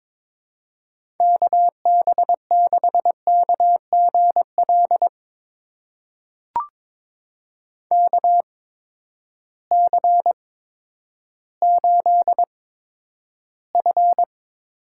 Morse code, heard as a keyed tone, about 23 wpm, spaced as for 11 wpm, 700 Hz. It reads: KB6KGL KC8F
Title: USA Call Signs - 6 Characters - 8x Character Spacing - 22wpm